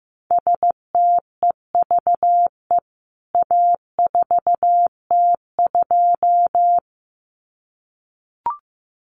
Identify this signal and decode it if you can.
Morse code, heard as a keyed tone, about 15 wpm, 700 Hz. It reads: STEVE A4T2
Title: CWT Exchange 15wpm